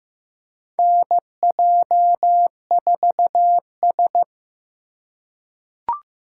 Morse code, heard as a keyed tone, about 15 wpm, 700 Hz. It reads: NJ4S
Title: USA Call Signs - 4 Characters - 3x Character Spacing - 15wpm